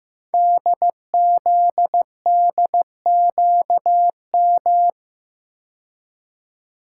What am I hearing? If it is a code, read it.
Morse code, heard as a keyed tone, about 15 wpm, 690 Hz. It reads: DZDQM
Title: Code Groups - 5 Letters - 15wpm